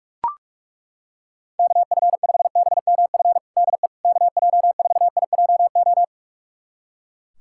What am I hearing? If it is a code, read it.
Morse code, heard as a keyed tone, about 42 wpm, 700 Hz. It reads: KF5BMF DE KJ4IJO
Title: USA Callsign DE Callsign - 6 Characters - 45wpm